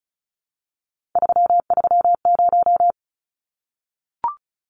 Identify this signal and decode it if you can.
Morse code, heard as a keyed tone, about 34 wpm, 700 Hz. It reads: 330
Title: Numbers - 3-Digit 35wpm